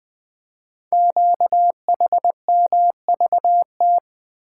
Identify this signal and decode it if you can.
Morse code, heard as a keyed tone, about 20 wpm, 700 Hz. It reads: QHMVT
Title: Code Groups - 5 Letters - 20wpm